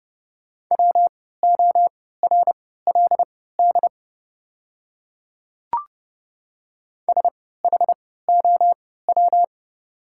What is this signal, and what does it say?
Morse code, heard as a keyed tone, about 30 wpm, spaced as for 17 wpm, 700 Hz. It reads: WORLD SHOW
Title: 3x Character Spacing - Top 200 Words - 30wpm